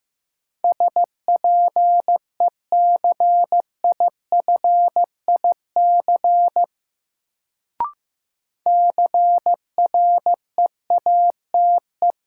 Morse code, heard as a keyed tone, about 15 wpm, 700 Hz. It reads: SPECIFIC CREATE
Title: Sentences from Top 1000 Words - Review Part 3 - New Word Drill - 15wpm